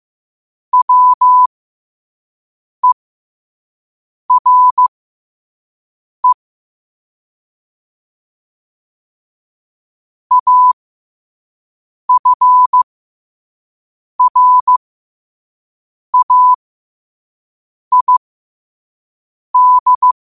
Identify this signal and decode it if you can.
Morse code, heard as a keyed tone, about 15 wpm, spaced as for 5 wpm, 1000 Hz. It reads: WERE AFRAID
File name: qotdfiles / 5wpm